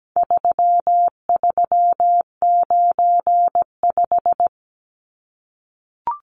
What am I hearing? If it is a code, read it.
Morse code, heard as a keyed tone, about 17 wpm, 700 Hz. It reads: 3395
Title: Numbers - 4-Digit 17wpm